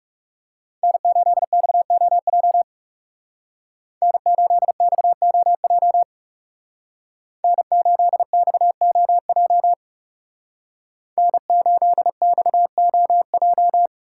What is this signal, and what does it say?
Morse code, heard as a keyed tone, about 37 wpm, 700 Hz. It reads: N8XOJ N8XOJ N8XOJ N8XOJ
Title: Speed Racing - USA 1x3 Call Signs 30wpm